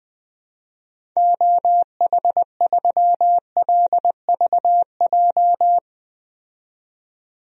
Morse code, heard as a keyed tone, about 20 wpm, 700 Hz. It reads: OH3LVJ
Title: DXCC-100 Call Signs - 6 Characters - 20wpm